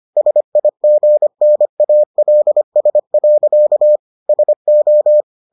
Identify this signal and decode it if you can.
Morse code, heard as a keyed tone, about 25 wpm, 600 Hz. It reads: SIGNALS. SO